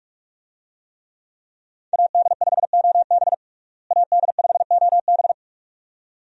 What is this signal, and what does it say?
Morse code, heard as a keyed tone, about 46 wpm, 700 Hz. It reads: AD5OB AD5OB
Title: USA Call Signs - 5 Characters - Repeated 2x - 45wpm